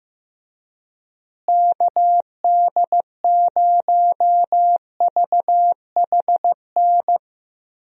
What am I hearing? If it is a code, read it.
Morse code, heard as a keyed tone, about 15 wpm, 700 Hz. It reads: KD0VHN
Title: USA Call Signs - 6 Characters - Repeated 2x - 15wpm